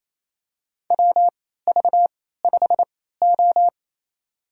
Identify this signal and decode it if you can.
Morse code, heard as a keyed tone, about 28 wpm, spaced as for 16 wpm, 700 Hz. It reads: WV5O
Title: USA Call Signs - 3x Character Spacing - 28wpm